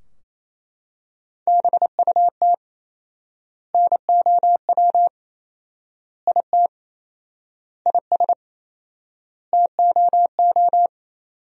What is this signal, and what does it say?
Morse code, heard as a keyed tone, about 28 wpm, 700 Hz. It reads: BUT NOW IT IS TOO
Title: 4x Word-Spacing - Sentences from Top 300 Words - 28wpm